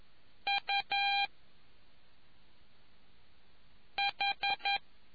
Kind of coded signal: Morse code